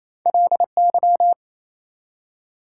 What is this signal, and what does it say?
Morse code, heard as a keyed tone, about 28 wpm, 700 Hz. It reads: LY
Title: Prefixes and Suffixes 28wpm